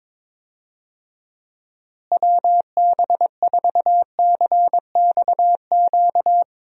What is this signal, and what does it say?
Morse code, heard as a keyed tone, about 22 wpm, 700 Hz. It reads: WB4CXQ
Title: USA Call Signs - 6 Characters - Repeated 2x - 22wpm